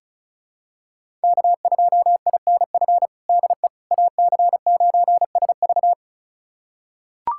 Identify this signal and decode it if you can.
Morse code, heard as a keyed tone, about 36 wpm, 700 Hz. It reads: K2INF DE AC9SV
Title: USA Callsign DE Callsign - 5 Characters - 35wpm